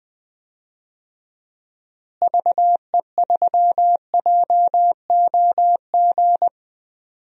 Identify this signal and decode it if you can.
Morse code, heard as a keyed tone, about 20 wpm, 700 Hz. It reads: VE3JOG
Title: DXCC-100 Call Signs - 6 Characters - Repeated 3x - 20wpm